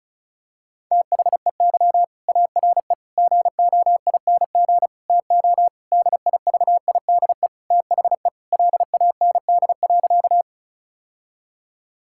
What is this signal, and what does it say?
Morse code, heard as a keyed tone, about 34 wpm, 700 Hz. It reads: THEY ARE GOING TO DIVIDE THE LAND.
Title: Sentences from Top 1600 Words 35wpm